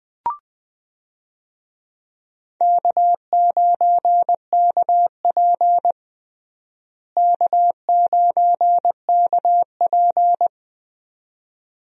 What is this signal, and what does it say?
Morse code, heard as a keyed tone, about 20 wpm, 700 Hz. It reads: K9KP K9KP
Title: USA Call Signs - 4 Characters - Repeated 2x - 20wpm